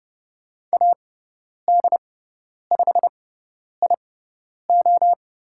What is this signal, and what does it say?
Morse code, heard as a keyed tone, about 30 wpm, spaced as for 10 wpm, 700 Hz. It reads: AD5IO
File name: Lesson 16g - Letter c - Call Signs 30wpm